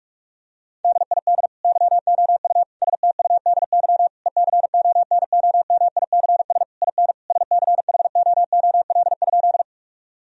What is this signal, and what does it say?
Morse code, heard as a keyed tone, about 41 wpm, 700 Hz. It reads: DID YOU STUDY ECONOMICS IN SCHOOL?
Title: Sentences from Top 1000 Words - Review Part 3 - 45wpm